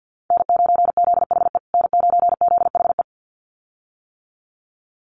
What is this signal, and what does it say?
Morse code, heard as a keyed tone, about 50 wpm, 700 Hz. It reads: N9ZHE N9ZHE
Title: USA Call Signs Repeated 2x 50wpm